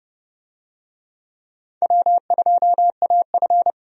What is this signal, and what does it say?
Morse code, heard as a keyed tone, about 30 wpm, 700 Hz. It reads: W2AF
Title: USA Call Signs - 4 Characters - 30wpm